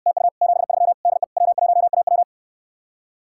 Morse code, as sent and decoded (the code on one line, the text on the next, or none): - ..- / --... ...-- / -.. / .-- .---- .- .--
TU 73 DE W1AW